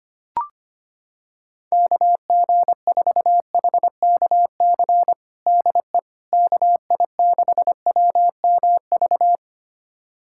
Morse code, heard as a keyed tone, about 25 wpm, 700 Hz. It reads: KG4HKC DE KI6WMV
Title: USA Callsign DE Callsign 25wpm